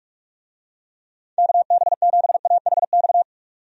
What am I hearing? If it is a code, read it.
Morse code, heard as a keyed tone, about 41 wpm, 700 Hz. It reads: KB7AHX
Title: USA Call Signs - 6 Characters - 45wpm